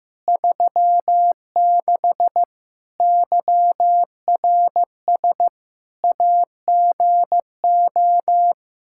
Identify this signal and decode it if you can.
Morse code, heard as a keyed tone, about 15 wpm, 700 Hz. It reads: 36 YRS AGO
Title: Practice QSO - Exchange 3 - More Info - 15wpm